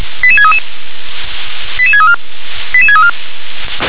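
Five-tone selective call (ZVEI), heard as quick runs of five tones, tones of about 70 ms. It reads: 88633 88634 88634
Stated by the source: Five-tone ZVEI selective call received on a software-defined radio.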